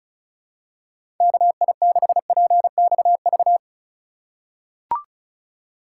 Morse code, heard as a keyed tone, about 36 wpm, 700 Hz. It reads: KI6PXV
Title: USA Call Signs - 6 Characters - 4x Character Spacing - 35wpm